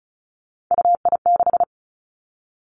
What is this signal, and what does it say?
Morse code, heard as a keyed tone, about 36 wpm, 700 Hz. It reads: UI6
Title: Three Challenging Letter-Number Combinations 35wpm